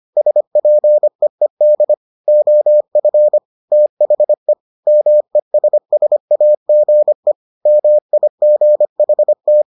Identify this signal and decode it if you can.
Morse code, heard as a keyed tone, about 25 wpm, 600 Hz. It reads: SPEED OF THE MESSAGE MIGHT